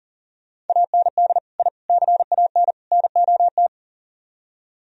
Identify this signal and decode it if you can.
Morse code, heard as a keyed tone, about 47 wpm, 700 Hz. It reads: AND I CAN NOT
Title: Speed-Racing - Sets of 4 Words 40wpm